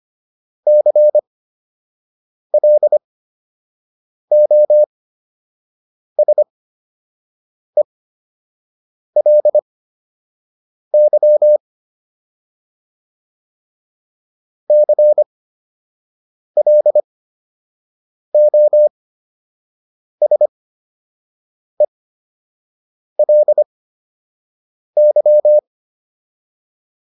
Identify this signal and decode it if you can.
Morse code, heard as a keyed tone, about 25 wpm, spaced as for 6 wpm, 600 Hz. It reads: CLOSELY CLOSELY